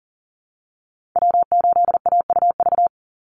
Morse code, heard as a keyed tone, about 40 wpm, 700 Hz. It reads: W8AUV
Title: USA Call Signs 40wpm